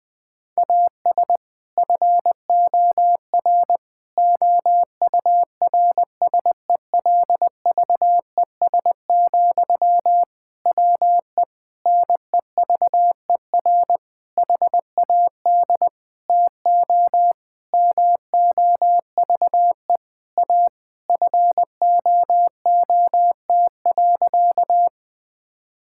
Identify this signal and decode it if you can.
Morse code, heard as a keyed tone, about 21 wpm, 700 Hz. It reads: AS FOR OURSELVES, WE NEVER HAD TO MOVE A FOOT.
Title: Sentences from Top 1600 Words 20wpm